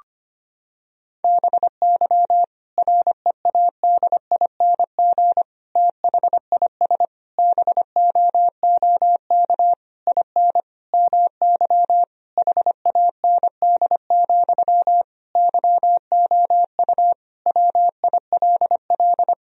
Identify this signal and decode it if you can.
Morse code, heard as a keyed tone, about 25 wpm, 700 Hz. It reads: BY READING THIS BOOK IN MY HAND, YOU WILL
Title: Sentences from Top 1000 Words - Review Part 3 - 25wpm